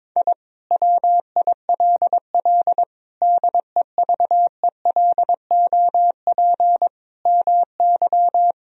Morse code, heard as a keyed tone, about 22 wpm, 700 Hz. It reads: I WILL DEVELOP MY
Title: Sentences from Top 1000 Words - Review Part 2 - 22wpm